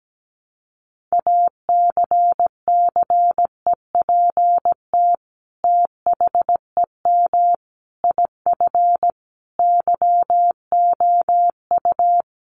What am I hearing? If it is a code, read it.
Morse code, heard as a keyed tone, about 17 wpm, 700 Hz. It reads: ACCEPT THEM IF YOU
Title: Sentences from Top 1300 Words 17wpm